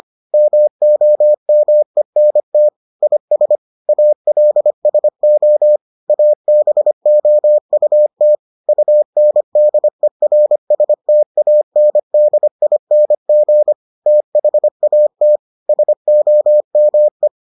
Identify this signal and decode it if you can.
Morse code, heard as a keyed tone, about 25 wpm, 600 Hz. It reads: MOMENT IS ALSO ABOUT UNDERSTANDING THAT SOME